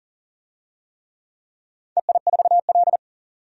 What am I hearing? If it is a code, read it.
Morse code, heard as a keyed tone, about 40 wpm, 700 Hz. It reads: EI4L
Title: DXCC-100 Call Signs - 4 Characters - 40wpm